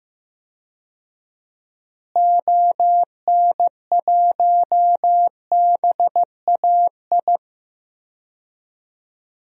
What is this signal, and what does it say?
Morse code, heard as a keyed tone, about 15 wpm, 700 Hz. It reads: ON1BAI
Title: DXCC-100 Call Signs - 6 Characters - Repeated 2x - 15wpm